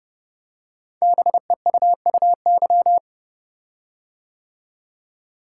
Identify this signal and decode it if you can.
Morse code, heard as a keyed tone, about 30 wpm, 700 Hz. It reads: BEUUY
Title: Code Groups - 5 Letters - 30wpm